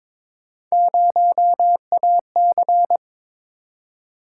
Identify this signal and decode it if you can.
Morse code, heard as a keyed tone, about 22 wpm, 700 Hz. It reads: 0AC
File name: practice / 22wpm / Three Letter-Number Combinations - 22wpm